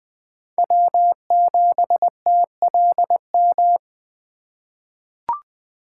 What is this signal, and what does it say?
Morse code, heard as a keyed tone, about 20 wpm, 700 Hz. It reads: W7TLM
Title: USA Call Signs - 5 Characters - 6x Character Spacing - 20wpm